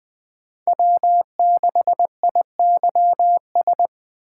Morse code, heard as a keyed tone, about 20 wpm, 700 Hz. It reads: W6IYS
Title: 2023 SOTA Chaser Call Signs - 20wpm